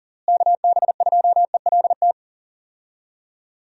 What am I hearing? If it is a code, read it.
Morse code, heard as a keyed tone, about 40 wpm, 700 Hz. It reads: KB2ELT
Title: USA Call Signs - 6 Characters - Repeated 2x - 40wpm